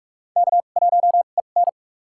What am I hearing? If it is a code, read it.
Morse code, heard as a keyed tone, about 46 wpm, spaced as for 32 wpm, 700 Hz. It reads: K1EN
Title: USA Call Signs - 2x Character Spacing - 45wpm